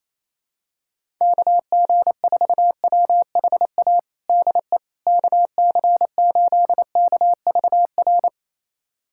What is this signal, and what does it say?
Morse code, heard as a keyed tone, about 27 wpm, 700 Hz. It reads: KG4WHA DE KC8KVR
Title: USA Callsign DE Callsign - 6 Characters - 28wpm